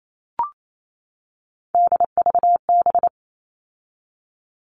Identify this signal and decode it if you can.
Morse code, heard as a keyed tone, about 28 wpm, 700 Hz. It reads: DVB
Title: All Cut Numbers - 3-Digit 28wpm